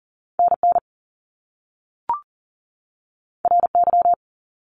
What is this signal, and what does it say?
Morse code, heard as a keyed tone, about 40 wpm, 700 Hz. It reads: NN RY
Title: Top Two Letter Combinations in Words 40wpm